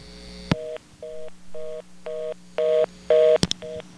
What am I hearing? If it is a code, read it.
Reorder tone (North America), 480 + 620 Hz.
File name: BellSouth Test ANAC